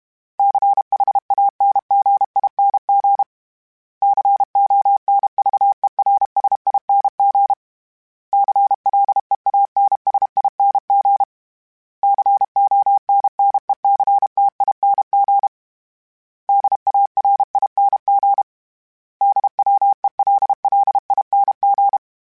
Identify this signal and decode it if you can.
Morse code, heard as a keyed tone, about 32 wpm, 800 Hz. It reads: CHANGING CONVERSING CLEANSING CONNECTING DARING DWELLING
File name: Special INGs-mp3s / Special INGs-mp3s / Sngl Ing @30